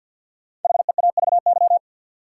Morse code, heard as a keyed tone, about 49 wpm, 700 Hz. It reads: HEAVY
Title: Sentences from Top 1100 Words - New Word Drill Alt - 50wpm